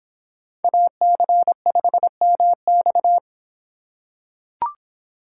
Morse code, heard as a keyed tone, about 25 wpm, 700 Hz. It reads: AC5MX